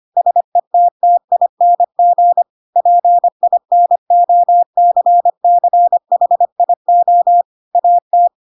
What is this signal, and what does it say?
Morse code, heard as a keyed tone, about 25 wpm, 700 Hz. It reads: SETTING PINOCCHIO AT
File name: Pinocchio / Pinocchio ch3 25 25 ik1hge